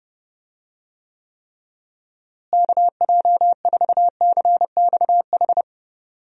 Morse code, heard as a keyed tone, about 32 wpm, 700 Hz. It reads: KJ4CXH
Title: USA Call Signs - 6 Characters - Repeated 3x - 30wpm